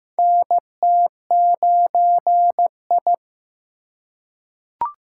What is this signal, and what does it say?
Morse code, heard as a keyed tone, about 15 wpm, 700 Hz. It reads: NT9I
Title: USA Call Signs - 4 Characters - 3x Character Spacing - 15wpm